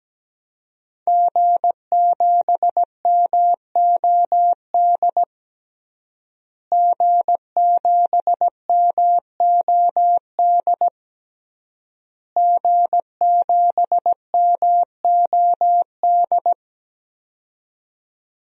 Morse code, heard as a keyed tone, about 17 wpm, 700 Hz. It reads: G7MOD G7MOD G7MOD
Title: DXCC-100 Call Signs - 5 Characters - Repeated 3x - 17wpm